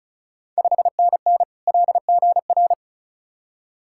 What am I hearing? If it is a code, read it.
Morse code, heard as a keyed tone, about 36 wpm, 700 Hz. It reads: 5NN LGR
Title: Ontario QSO Party Exchange 35wpm